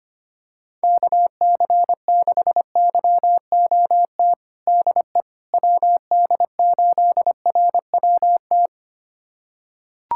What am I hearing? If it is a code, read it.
Morse code, heard as a keyed tone, about 25 wpm, 700 Hz. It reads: KC6YOT DE WD8RWT
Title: USA Callsign DE Callsign 25wpm